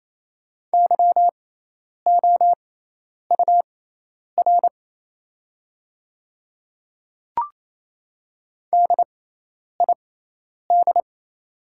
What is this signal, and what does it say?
Morse code, heard as a keyed tone, about 28 wpm, spaced as for 10 wpm, 690 Hz. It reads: YOUR DID
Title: 6x Character Spacing - Top 100 Words - 28wpm